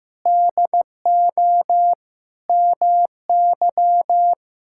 Morse code, heard as a keyed tone, about 15 wpm, 700 Hz. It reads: DO MY
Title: Sentences from Top 1000 Words - Review Part 6 - 15wpm